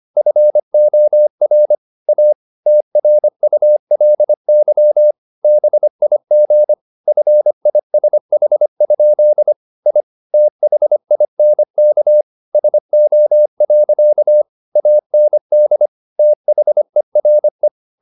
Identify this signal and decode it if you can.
Morse code, heard as a keyed tone, about 25 wpm, 600 Hz. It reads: FOR A TRULY BIG FISH? I THINK SO. AND THERE
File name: the old man and the sea part 1 25wpm